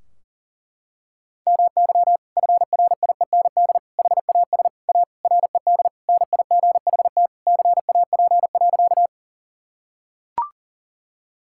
Morse code, heard as a keyed tone, about 40 wpm, 700 Hz. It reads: MY FRIEND HAS A RED NIGHT CAP.
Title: Sentences from Top 1600 Words 40wpm